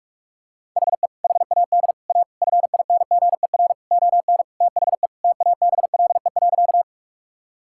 Morse code, heard as a keyed tone, about 47 wpm, 700 Hz. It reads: HE HAD A FINGER ON THE TABLE.
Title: Sentences from Top 1600 Words 45wpm